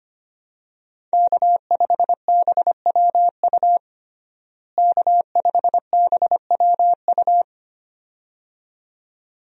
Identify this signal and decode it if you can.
Morse code, heard as a keyed tone, about 25 wpm, 700 Hz. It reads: K5BWU K5BWU
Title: USA Call Signs - 5 Characters - Repeated 2x - 25wpm